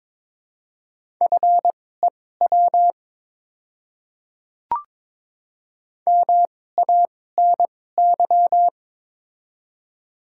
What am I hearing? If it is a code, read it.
Morse code, heard as a keyed tone, about 22 wpm, spaced as for 16 wpm, 700 Hz. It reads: FEW MANY
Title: 2x Character Spacing - Top 300 Words - 22wpm